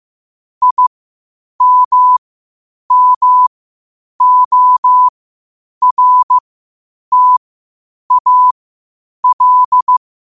Morse code, heard as a keyed tone, about 15 wpm, spaced as for 8 wpm, 1000 Hz. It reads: IMMORTAL
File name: qotdfiles / 8wpm